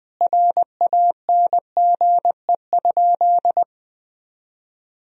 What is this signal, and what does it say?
Morse code, heard as a keyed tone, about 20 wpm, 700 Hz. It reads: RANGE?
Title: Sentences from Top 1000 Words - Review Part 4 - 20wpm